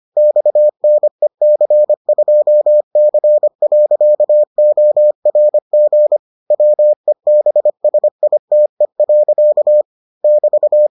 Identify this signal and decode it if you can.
Morse code, heard as a keyed tone, about 25 wpm, 600 Hz. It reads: XNEC2C.ORG WEBSITE. <BT>